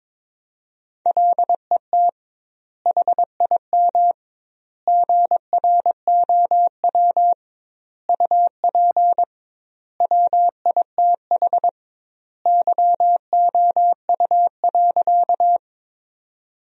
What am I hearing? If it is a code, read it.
Morse code, heard as a keyed tone, about 21 wpm, 700 Hz. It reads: LET HIM GROW UP WITH YOU.
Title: 2x Word-Spacing - Sentences from Top 300 Words - 22wpm